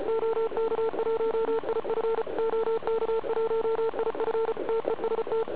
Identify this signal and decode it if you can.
Morse code, heard as a keyed tone, about 34 wpm, 450 Hz, with noise. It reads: OK1IF OK1IF TEST
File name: ARCHIV / zvuk2 / CW RUN